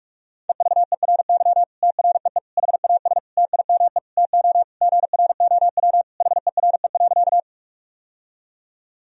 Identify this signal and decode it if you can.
Morse code, heard as a keyed tone, about 42 wpm, 700 Hz. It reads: EVERY TREE HAS TIME TO GROW HERE.
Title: Sentences from Top 1000 Words - Review Part 1 - 45wpm